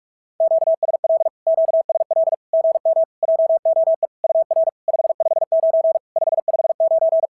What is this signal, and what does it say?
Morse code, heard as a keyed tone, about 44 wpm, 690 Hz. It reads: QSL QSL GM JOE UR 559 559
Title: SOTA QSO Practice 45wpm